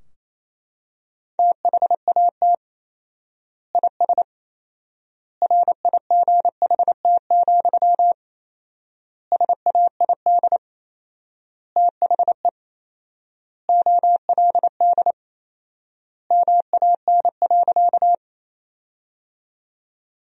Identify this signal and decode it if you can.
Morse code, heard as a keyed tone, about 28 wpm, 700 Hz. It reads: THAT IS RIGHT, SAID THE OLD MAN.
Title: 4x Word-Spacing - Sentences from Top 200 Words - 28wpm